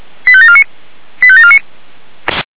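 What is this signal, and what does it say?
Five-tone selective call (ZVEI), heard as quick runs of five tones, tones of about 70 ms. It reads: 85749 85749
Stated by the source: ZVEI five-tone call received by SDR.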